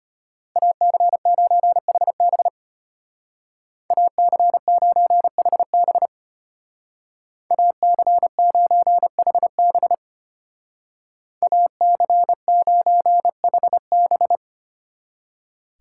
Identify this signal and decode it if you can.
Morse code, heard as a keyed tone, about 33 wpm, 700 Hz. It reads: AC9HB AC9HB AC9HB AC9HB